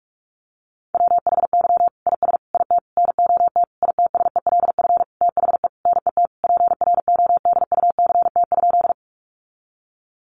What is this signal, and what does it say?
Morse code, heard as a keyed tone, about 44 wpm, 700 Hz. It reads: WHY IS IT NOT ITSELF THE NET PRODUCT?
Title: Sentences from Top 1400 Words 45wpm